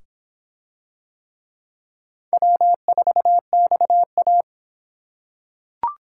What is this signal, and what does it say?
Morse code, heard as a keyed tone, about 24 wpm, 700 Hz. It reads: W4XA